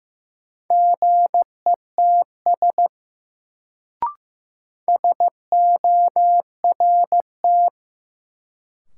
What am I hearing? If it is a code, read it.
Morse code, heard as a keyed tone, about 16 wpm, 700 Hz. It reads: GETS SORT
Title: Sentences from Top 1300 Words - New Word Drill Alt - 15wpm